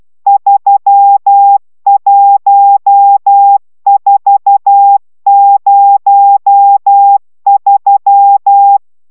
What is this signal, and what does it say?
Morse code, heard as a keyed tone, about 12 wpm, 800 Hz. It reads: 31403